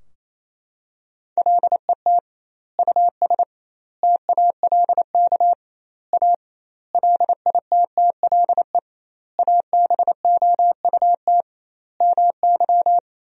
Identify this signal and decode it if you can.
Morse code, heard as a keyed tone, about 28 wpm, 700 Hz. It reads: LET US TALK A LITTLE ABOUT MY
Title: 2x Word-Spacing - Sentences from Top 400 Words - 28wpm